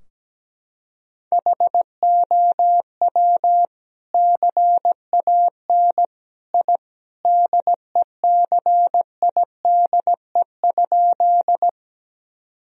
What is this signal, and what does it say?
Morse code, heard as a keyed tone, about 17 wpm, 700 Hz. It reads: HOW CAN I DECIDE?
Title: Sentences from Top 1000 Words - Review Part 3 - 17wpm